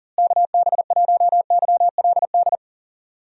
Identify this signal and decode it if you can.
Morse code, heard as a keyed tone, about 40 wpm, 700 Hz. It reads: KB1YLD
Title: USA Call Signs - 6 Characters - Repeated 2x - 40wpm